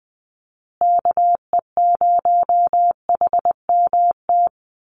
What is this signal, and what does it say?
Morse code, heard as a keyed tone, about 19 wpm, 700 Hz. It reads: KE0HMT
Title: USA Call Signs Repeated 2x 20wpm